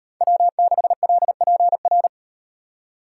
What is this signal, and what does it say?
Morse code, heard as a keyed tone, about 36 wpm, 700 Hz. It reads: W6LPR